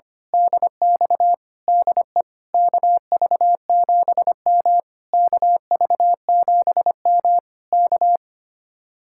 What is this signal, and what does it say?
Morse code, heard as a keyed tone, about 24 wpm, 700 Hz. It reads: DX DE KV7M KV7M K
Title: Speed-Racing Common QSO Phrases Encore - 25wpm